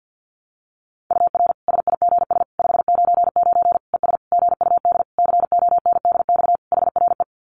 Morse code, heard as a keyed tone, about 50 wpm, 700 Hz. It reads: UR SIGS 599 ES GUD CONDX HRE
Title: Common QSO Phrases 50wpm